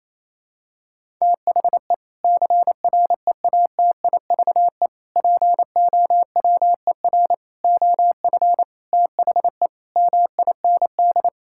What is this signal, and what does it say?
Morse code, heard as a keyed tone, about 28 wpm, 700 Hz. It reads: THE CREATIVE POWER OF THE MIND
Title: Sentences from Top 1000 Words - Review Part 5 - 28wpm